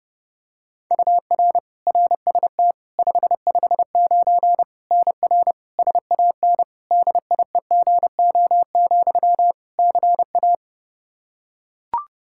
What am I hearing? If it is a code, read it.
Morse code, heard as a keyed tone, about 32 wpm, 700 Hz. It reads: UR RST 559 NR SAN DIEGO, CA
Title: Practice QSO - Exchange 1 - Beginning - 30wpm